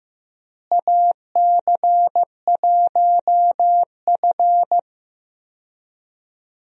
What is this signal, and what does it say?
Morse code, heard as a keyed tone, about 15 wpm, 700 Hz. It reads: AC1F
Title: USA Call Signs - 4 Characters - Repeated 2x - 15wpm